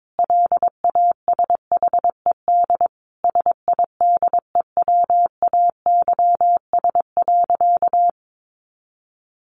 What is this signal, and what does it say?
Morse code, heard as a keyed tone, about 22 wpm, 700 Hz. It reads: LASHED SIDEWAYS.
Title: A Sound Of Thunder 22wpm